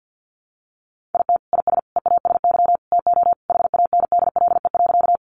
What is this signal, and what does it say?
Morse code, heard as a keyed tone, about 48 wpm, 700 Hz. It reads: IT IS EASY TO HANDLE.
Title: Sentences from Top 1100 Words 50wpm